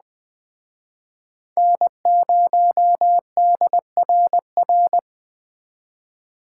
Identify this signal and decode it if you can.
Morse code, heard as a keyed tone, about 21 wpm, 700 Hz. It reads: N0DRR
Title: Speed Racing - USA 1x3 Call Signs 17wpm